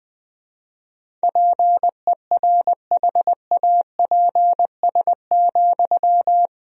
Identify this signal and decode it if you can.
Morse code, heard as a keyed tone, about 20 wpm, 700 Hz. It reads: PERHAPS,
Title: Sentences from Top 500 Words and Abbreviations 20wpm